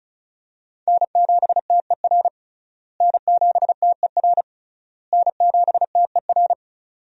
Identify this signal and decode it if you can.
Morse code, heard as a keyed tone, about 34 wpm, 700 Hz. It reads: N7TER N7TER N7TER
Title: USA Call Signs - 5 Characters - Repeated 3x - 35wpm